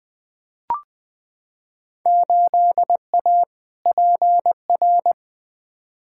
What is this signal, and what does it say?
Morse code, heard as a keyed tone, about 20 wpm, 700 Hz. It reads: 8A PR
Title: Field Day Exchanges Encore - 20wpm